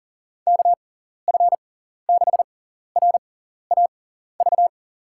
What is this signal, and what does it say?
Morse code, heard as a keyed tone, about 40 wpm, spaced as for 14 wpm, 700 Hz. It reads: KF6RAV
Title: USA Call Signs - 6 Characters - 6x Character Spacing - 40wpm